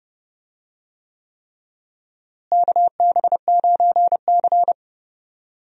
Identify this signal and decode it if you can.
Morse code, heard as a keyed tone, about 30 wpm, 700 Hz. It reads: KB9C
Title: USA Call Signs - 4 Characters - Repeated 3x - 30wpm